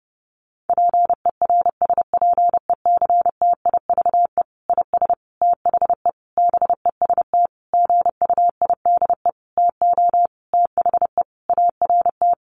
Morse code, heard as a keyed tone, about 30 wpm, 700 Hz. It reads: PERSPECTIVE IS THE BEST GUIDE TO THE ART
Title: Sentences from Top 1200 Words 30wpm